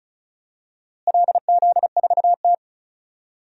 Morse code, heard as a keyed tone, about 34 wpm, 700 Hz. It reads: LZ4T
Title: DXCC-100 Call Signs - 4 Characters - Repeated 2x - 35wpm